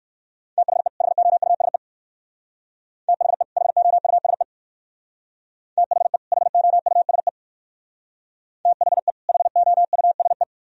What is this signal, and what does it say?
Morse code, heard as a keyed tone, about 56 wpm, 700 Hz. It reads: THE HOUSE THE HOUSE THE HOUSE THE HOUSE
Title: Speed-Racing - Sets of 2 Words 45wpm